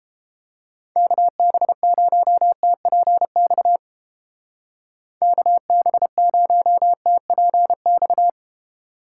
Morse code, heard as a keyed tone, about 31 wpm, 700 Hz. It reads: KB0TPX KB0TPX